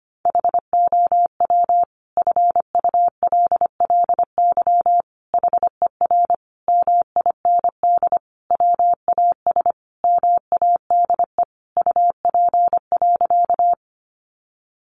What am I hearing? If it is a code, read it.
Morse code, heard as a keyed tone, about 25 wpm, 700 Hz. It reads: HOW FULLY HER MIND WAS MADE UP.
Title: Sentences from Top 1300 Words 25wpm